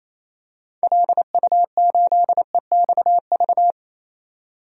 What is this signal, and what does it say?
Morse code, heard as a keyed tone, about 28 wpm, 700 Hz. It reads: LU8EXV
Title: DXCC-100 Call Signs - 6 Characters - 28wpm